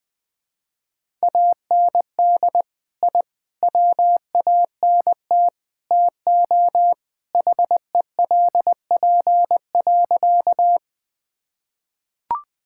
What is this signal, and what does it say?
Morse code, heard as a keyed tone, about 21 wpm, 700 Hz. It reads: AND I WANT TO HELP.
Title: Sentences with 4-Word Phrases - 20wpm